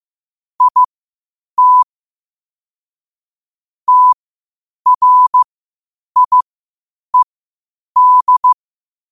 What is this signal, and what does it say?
Morse code, heard as a keyed tone, about 15 wpm, spaced as for 8 wpm, 1000 Hz. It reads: IT TRIED
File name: qotdfiles / 8wpm